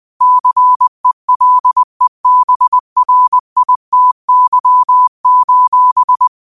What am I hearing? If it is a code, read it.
Morse code, heard as a keyed tone, about 20 wpm, 1000 Hz. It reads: CELEBRITY:
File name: qotdfiles / 20wpm